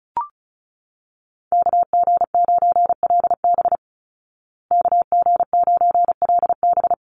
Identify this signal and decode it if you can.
Morse code, heard as a keyed tone, about 34 wpm, 700 Hz. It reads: KG9LB KG9LB
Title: Call Signs Repeated 2x - 35wpm